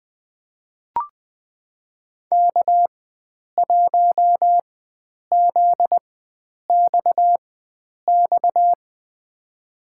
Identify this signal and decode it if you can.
Morse code, heard as a keyed tone, about 20 wpm, spaced as for 9 wpm, 700 Hz. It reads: K1ZXX
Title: USA Call Signs - 5 Characters - 4x Character Spacing - 20wpm